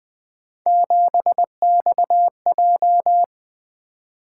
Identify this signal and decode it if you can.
Morse code, heard as a keyed tone, about 20 wpm, 700 Hz. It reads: 7XJ
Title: Three Letter-Number Combinations - 20wpm